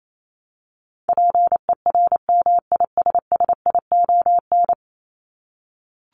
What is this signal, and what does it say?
Morse code, heard as a keyed tone, about 28 wpm, 700 Hz. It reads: PERMISSION
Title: Top 1501-1750 Words 28wpm